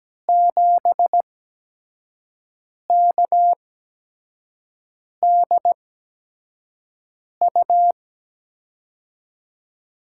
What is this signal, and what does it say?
Morse code, heard as a keyed tone, about 17 wpm, spaced as for 5 wpm, 700 Hz. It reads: 7KDU
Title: USA Call Signs - 5 Characters - 8x Character Spacing - 17wpm